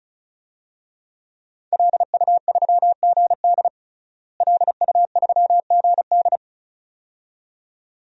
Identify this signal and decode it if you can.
Morse code, heard as a keyed tone, about 34 wpm, 700 Hz. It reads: LU3GD LU3GD
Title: DXCC-100 Call Signs - 5 Characters - Repeated 2x - 35wpm